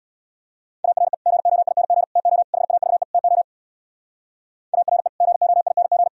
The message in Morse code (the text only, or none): ARE GOING TO HAVE TO ARE GOING